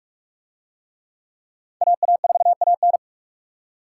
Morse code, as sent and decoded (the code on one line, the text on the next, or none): .- .- ....- .- -.
AA4AN